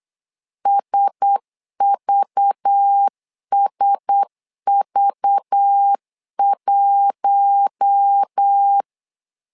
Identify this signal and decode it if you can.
Morse code, heard as a keyed tone, about 8 wpm, 800 Hz. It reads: SVSV1
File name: LEZIONI / 5 LBVC / 5 LBV 8wpm 2837